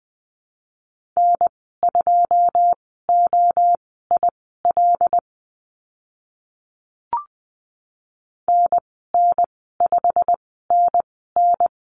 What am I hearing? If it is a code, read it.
Morse code, heard as a keyed tone, about 20 wpm, spaced as for 15 wpm, 700 Hz. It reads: N2OIL NN5NN
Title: 088 Lesson 15g - Number 5 - Call Signs 20wpm